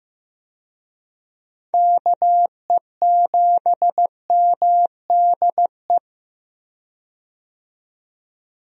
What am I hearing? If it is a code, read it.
Morse code, heard as a keyed tone, about 15 wpm, 700 Hz. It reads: KE7MDE
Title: USA Call Signs - 6 Characters - 15wpm